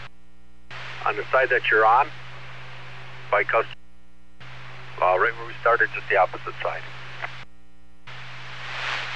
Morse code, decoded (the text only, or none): none